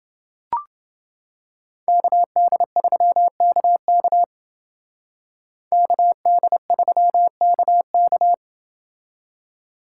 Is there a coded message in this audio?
Yes: Morse code from 1.9 to 8.3 s, heard as a keyed tone, about 28 wpm, 700 Hz. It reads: KD3KK KD3KK